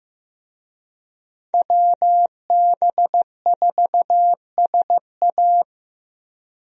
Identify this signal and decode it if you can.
Morse code, heard as a keyed tone, about 16 wpm, 700 Hz. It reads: WB4SA
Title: USA Call Signs - 5 Characters - Repeated 2x - 15wpm